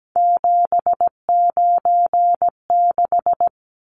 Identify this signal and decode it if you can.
Morse code, heard as a keyed tone, about 17 wpm, 700 Hz. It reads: 796
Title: Numbers - 3-Digit 17wpm